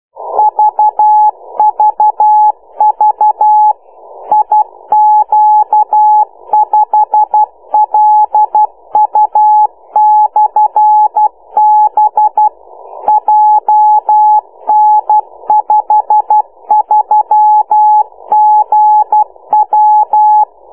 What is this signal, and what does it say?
Morse code, heard as a keyed tone, about 12 wpm, 850 Hz. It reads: VVV IQ5LU/B JN53GW